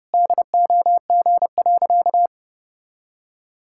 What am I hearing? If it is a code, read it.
Morse code, heard as a keyed tone, about 32 wpm, 700 Hz. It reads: DOG.
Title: Pangram Challenge v2 - 30wpm